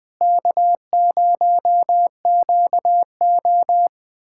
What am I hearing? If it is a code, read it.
Morse code, heard as a keyed tone, about 21 wpm, 700 Hz. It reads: K0QO
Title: USA Callsign DE Callsign - 4 Characters - 20wpm